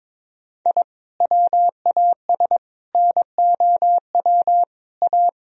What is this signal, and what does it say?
Morse code, heard as a keyed tone, about 22 wpm, 700 Hz. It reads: I WAS NOW A
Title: Sentences from Top 1000 Words - Review Part 2 - 22wpm